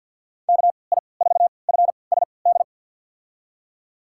Morse code, heard as a keyed tone, about 47 wpm, spaced as for 28 wpm, 690 Hz. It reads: KI4FSD
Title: USA Call Signs - 6 Characters - 3x Character Spacing - 50wpm